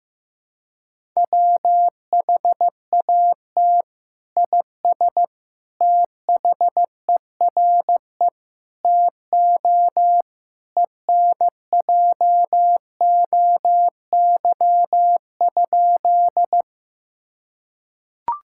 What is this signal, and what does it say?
Morse code, heard as a keyed tone, about 15 wpm, 700 Hz. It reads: WHAT IS THERE TO ENJOY?
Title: Sentences from Top 1000 Words - Review Part 5 - 15wpm